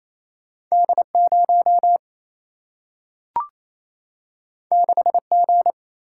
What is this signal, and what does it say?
Morse code, heard as a keyed tone, about 28 wpm, 700 Hz. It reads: D 0 6 G
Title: Single-Letter Single-Number 28wpm